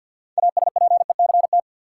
Morse code, heard as a keyed tone, about 53 wpm, 700 Hz. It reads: ASPECT
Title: Sentences from Top 1300 Words - New Word Drill - 50wpm